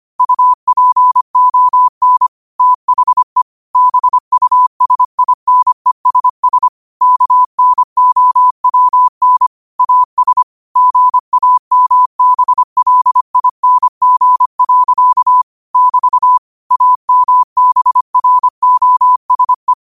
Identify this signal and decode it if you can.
Morse code, heard as a keyed tone, about 25 wpm, 1000 Hz. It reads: UPON THE BUSINESS KNOWN AS GAMBLING. <BT> AMBROSE
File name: qotdfiles / 25wpm